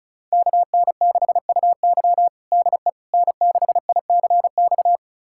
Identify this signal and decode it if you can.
Morse code, heard as a keyed tone, about 36 wpm, 700 Hz. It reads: KN6UY DE N6ICX
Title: USA Callsign DE Callsign - 5 Characters - 35wpm